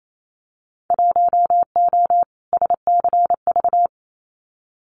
Morse code, heard as a keyed tone, about 28 wpm, 700 Hz. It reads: 1O SCV
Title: Winter Field Day Exchanges 28wpm